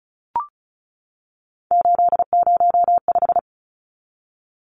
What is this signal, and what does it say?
Morse code, heard as a keyed tone, about 34 wpm, 700 Hz. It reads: 805
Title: Numbers - 3-Digit 35wpm